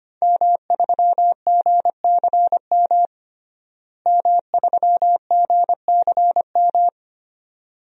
Morse code, heard as a keyed tone, about 25 wpm, 700 Hz. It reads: M3GCM M3GCM
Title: DXCC-100 Call Signs - 5 Characters - Repeated 2x - 25wpm